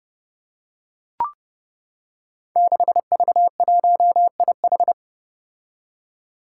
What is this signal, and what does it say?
Morse code, heard as a keyed tone, about 30 wpm, 700 Hz. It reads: 6V1IH
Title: Five Challenging Letter-Number Combinations 30wpm